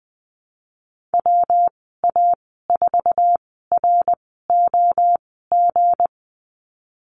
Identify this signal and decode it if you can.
Morse code, heard as a keyed tone, about 20 wpm, spaced as for 15 wpm, 700 Hz. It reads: WA4ROG
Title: 193 Lesson 31g - Letter v - Call Signs 20wpm